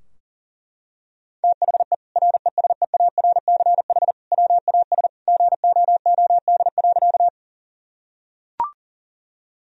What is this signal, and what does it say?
Morse code, heard as a keyed tone, about 40 wpm, 700 Hz. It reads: THE RESEARCH WAS GOOD.
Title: Sentences from Top 1000 Words - Review Part 4 - 40wpm